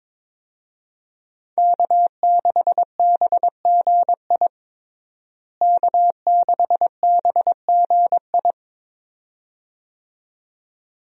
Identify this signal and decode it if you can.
Morse code, heard as a keyed tone, about 22 wpm, 700 Hz. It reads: K6BGI K6BGI
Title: USA Call Signs - 5 Characters - Repeated 2x - 22wpm